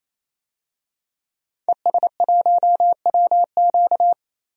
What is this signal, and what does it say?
Morse code, heard as a keyed tone, about 28 wpm, 700 Hz. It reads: ES1WQ
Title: DXCC-100 Call Signs - 5 Characters - 28wpm